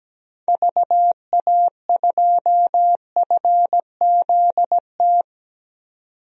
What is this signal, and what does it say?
Morse code, heard as a keyed tone, about 17 wpm, 700 Hz. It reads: VA2FZT
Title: DXCC-100 Call Signs - 6 Characters - 17wpm